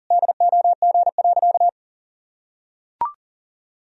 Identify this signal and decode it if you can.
Morse code, heard as a keyed tone, about 40 wpm, 700 Hz. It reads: DOG.
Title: Sentences from Top 1000 Words - Review Part 2 - 40wpm